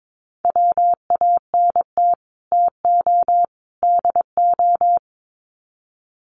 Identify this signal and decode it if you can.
Morse code, heard as a keyed tone, about 22 wpm, 700 Hz. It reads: WANT TO DO
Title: Sets of 3 Words 22wpm